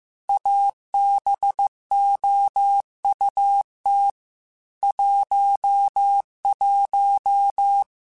Morse code, heard as a keyed tone, about 15 wpm, 800 Hz. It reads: ABOUT 11
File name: QSOs / QSOs-mp3s / WA7QOX 1@13